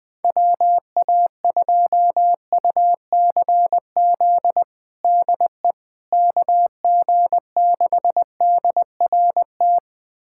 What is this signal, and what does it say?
Morse code, heard as a keyed tone, about 20 wpm, 700 Hz. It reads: WA2UCZ DE KG6DRT
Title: USA Callsign DE Callsign - 6 Characters - 20wpm